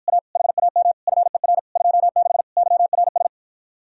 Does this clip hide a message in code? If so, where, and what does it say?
Yes: Morse code from 0.1 to 3.3 s, heard as a keyed tone, about 54 wpm, 700 Hz. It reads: A HAM FER 26 YRS